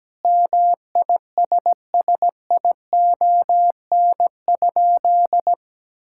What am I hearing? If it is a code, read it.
Morse code, heard as a keyed tone, about 17 wpm, 700 Hz. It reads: MISSION?
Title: Sentences from Top 1600 Words 17wpm